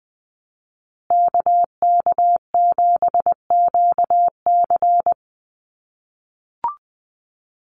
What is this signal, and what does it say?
Morse code, heard as a keyed tone, about 20 wpm, 700 Hz. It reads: KK7QC
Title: USA Call Signs Repeated 2x 20wpm